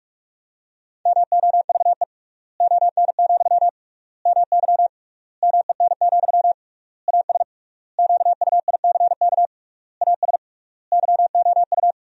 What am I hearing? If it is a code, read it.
Morse code, heard as a keyed tone, about 46 wpm, 700 Hz. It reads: MOVE ON, MY MEN, AS QUICK AS YOU
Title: 3x Word-Spacing - Sentences from Top 500 Words - 45wpm